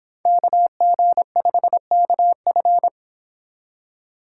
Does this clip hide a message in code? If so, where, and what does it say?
Yes: Morse code from 0.3 to 2.9 s, heard as a keyed tone, about 25 wpm, 700 Hz. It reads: KG5KF